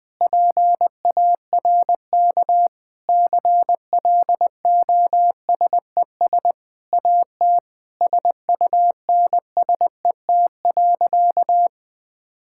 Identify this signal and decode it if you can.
Morse code, heard as a keyed tone, about 20 wpm, 700 Hz. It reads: PARK CLOSES AT SUNSET.
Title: Sentences with 7-Word Phrases - Spoken Beginning - 20wpm